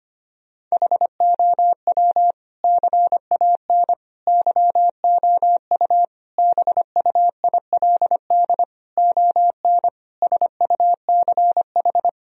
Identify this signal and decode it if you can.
Morse code, heard as a keyed tone, about 25 wpm, 700 Hz. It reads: HOW CAN YOU BUILD ON SUCH
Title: Sentences from Top 1000 Words - Review Part 1 - 25wpm